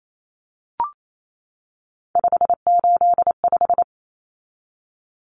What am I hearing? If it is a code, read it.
Morse code, heard as a keyed tone, about 28 wpm, 700 Hz. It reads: 585
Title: Numbers - 3-Digit 28wpm